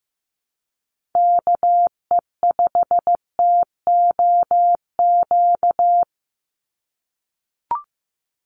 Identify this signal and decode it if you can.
Morse code, heard as a keyed tone, about 15 wpm, 700 Hz. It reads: KE5TOQ
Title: USA Call Signs 15wpm